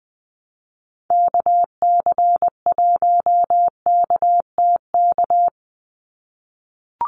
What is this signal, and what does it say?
Morse code, heard as a keyed tone, about 20 wpm, 700 Hz. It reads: KC1KTK
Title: USA Call Signs 20wpm